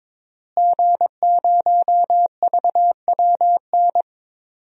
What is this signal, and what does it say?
Morse code, heard as a keyed tone, about 22 wpm, 700 Hz. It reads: G0VWN
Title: DXCC-100 Call Signs - 5 Characters - Repeated 2x - 22wpm